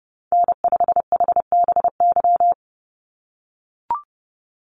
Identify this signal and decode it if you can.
Morse code, heard as a keyed tone, about 30 wpm, 700 Hz. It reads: N5HBY
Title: USA Call Signs 30wpm